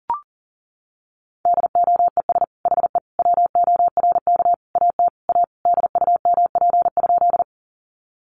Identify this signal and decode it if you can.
Morse code, heard as a keyed tone, about 40 wpm, 700 Hz. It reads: DOES HE WORK AT A DUMP?
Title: Sentences from Top 1300 Words 40wpm